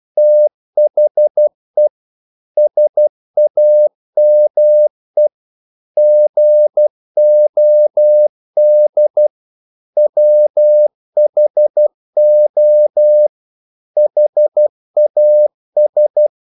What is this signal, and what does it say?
Morse code, heard as a keyed tone, about 12 wpm, 600 Hz. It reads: THE SAME GOD WHO HAS
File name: CW / Practice / EnglishQuotations0112